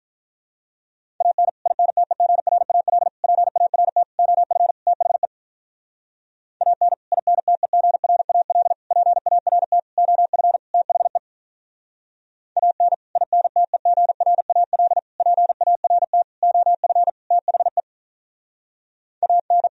Morse code, heard as a keyed tone, about 46 wpm, 700 Hz. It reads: AN INTEGRAL PART OF THE AN INTEGRAL PART OF THE AN INTEGRAL PART OF THE AN